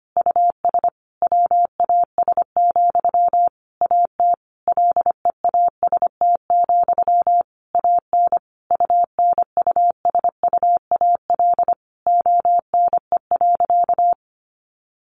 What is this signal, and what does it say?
Morse code, heard as a keyed tone, about 25 wpm, 700 Hz. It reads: US WAS, AT LEAST, AN UNUSUAL ONE.
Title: Sentences from Top 1400 Words 25wpm